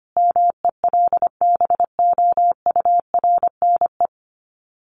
Morse code, heard as a keyed tone, about 25 wpm, 700 Hz. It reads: MELBOURNE
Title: Common QSO Phrases 25wpm